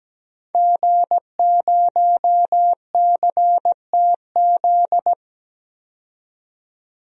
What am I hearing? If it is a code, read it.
Morse code, heard as a keyed tone, about 17 wpm, 700 Hz. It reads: G0CTZ
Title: DXCC-100 Call Signs - 5 Characters - 17wpm